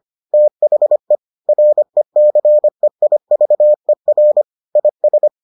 Morse code, heard as a keyed tone, about 25 wpm, 600 Hz. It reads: THE RECEIVER IS